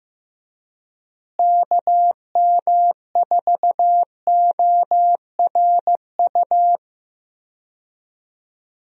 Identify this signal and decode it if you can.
Morse code, heard as a keyed tone, about 15 wpm, 700 Hz. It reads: KM4ORU
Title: USA Call Signs - 6 Characters - Repeated 2x - 15wpm